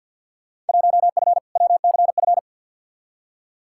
Morse code, heard as a keyed tone, about 47 wpm, 700 Hz. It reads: 1F WCF
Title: Field Day Exchanges - 50wpm